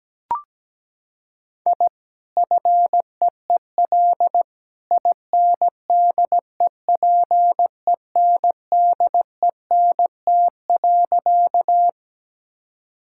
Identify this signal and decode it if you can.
Morse code, heard as a keyed tone, about 17 wpm, 700 Hz. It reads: I FEEL INDEPENDENT.
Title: Sentences from Top 1000 Words - Review Part 6 - 17wpm